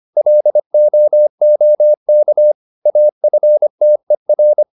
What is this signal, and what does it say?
Morse code, heard as a keyed tone, about 25 wpm, 600 Hz. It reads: LOOK AFTER